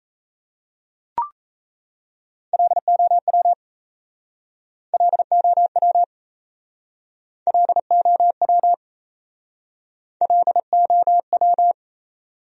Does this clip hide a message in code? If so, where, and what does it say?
Yes: Morse code from 2.5 to 11.7 s, heard as a keyed tone, about 40 wpm, 700 Hz. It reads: LOW LOW LOW LOW